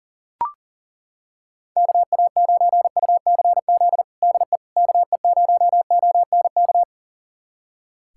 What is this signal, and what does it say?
Morse code, heard as a keyed tone, about 40 wpm, 700 Hz. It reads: KA9UCZ DE KE0ONK
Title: USA Callsign DE Callsign - 6 Characters - 40wpm